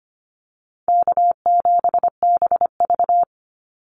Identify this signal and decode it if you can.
Morse code, heard as a keyed tone, about 25 wpm, 700 Hz. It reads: K7BV
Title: Easy Call Signs Repeated 2x - 25wpm